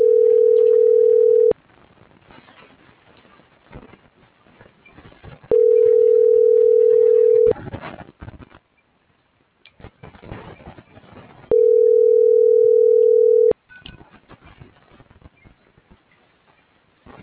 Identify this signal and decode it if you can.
Ringback tone (North America), 440 + 480 Hz.